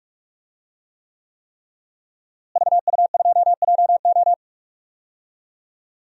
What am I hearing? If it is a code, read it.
Morse code, heard as a keyed tone, about 47 wpm, 700 Hz. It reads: VU2JO
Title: DXCC-100 Call Signs - 5 Characters - Repeated 3x - 45wpm